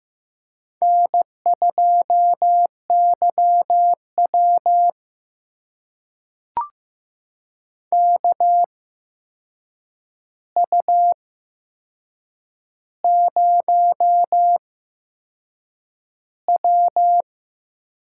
Morse code, heard as a keyed tone, about 15 wpm, spaced as for 6 wpm, 700 Hz. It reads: N2YW KU0W
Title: USA Call Signs - 4 Characters - 8x Character Spacing - 15wpm